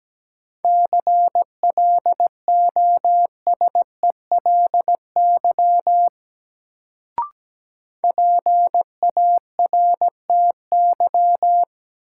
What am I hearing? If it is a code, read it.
Morse code, heard as a keyed tone, about 17 wpm, 700 Hz. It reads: CLOSELY PARTY
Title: Sentences from Top 1400 Words - New Word Drill - 17wpm